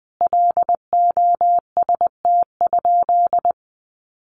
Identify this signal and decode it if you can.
Morse code, heard as a keyed tone, about 20 wpm, 700 Hz. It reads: LOST?
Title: Sentences from Top 1400 Words 20wpm